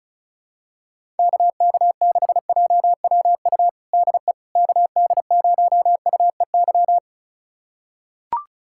Morse code, heard as a keyed tone, about 36 wpm, 700 Hz. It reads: KK6JWU DE KD0UEY
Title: USA Callsign DE Callsign 35wpm